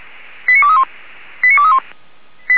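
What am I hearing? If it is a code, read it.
Five-tone selective call (ZVEI), heard as quick runs of five tones, tones of about 70 ms. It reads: 89231 89231
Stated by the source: ZVEI five-tone call received by SDR.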